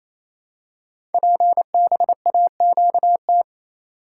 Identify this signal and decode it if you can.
Morse code, heard as a keyed tone, about 27 wpm, 700 Hz. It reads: PBAQT
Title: Code Groups - 5 Letters - 28wpm